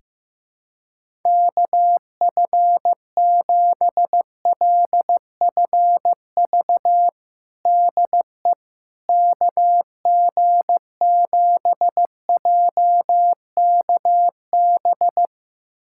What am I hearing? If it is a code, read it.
Morse code, heard as a keyed tone, about 15 wpm, 700 Hz. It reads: KF7LFV DE KG7JKB
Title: USA Callsign DE Callsign 15wpm